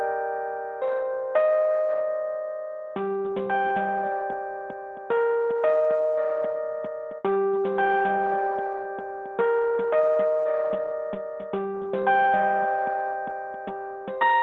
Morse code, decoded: none